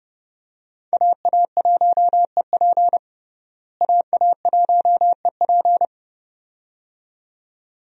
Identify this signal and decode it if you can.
Morse code, heard as a keyed tone, about 32 wpm, 700 Hz. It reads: AA1EP AA1EP
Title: USA Call Signs - 5 Characters - Repeated 2x - 30wpm